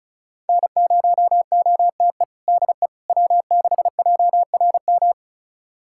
Morse code, heard as a keyed tone, about 34 wpm, 700 Hz. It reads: N0OTE DE W6JRM
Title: USA Callsign DE Callsign - 5 Characters - 35wpm